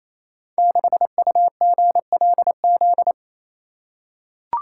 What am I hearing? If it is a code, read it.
Morse code, heard as a keyed tone, about 27 wpm, 700 Hz. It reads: 6UGLZ
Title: Code Groups - 5 Letter:Numbers - 28wpm